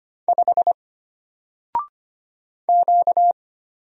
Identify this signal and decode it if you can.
Morse code, heard as a keyed tone, about 25 wpm, 700 Hz. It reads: 5 Q
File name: Single-Letter-Number-25wpm